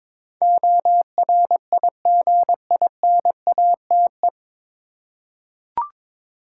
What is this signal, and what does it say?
Morse code, heard as a keyed tone, about 22 wpm, 700 Hz. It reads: ORIGINATE
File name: 9 Letter Words - First 3 Letters Given - 22wpm